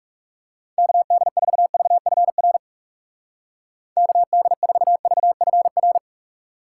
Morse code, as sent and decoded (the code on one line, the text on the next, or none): -.- -.. ....- ...- ..-. .-. / -.- -.. ....- ...- ..-. .-.
KD4VFR KD4VFR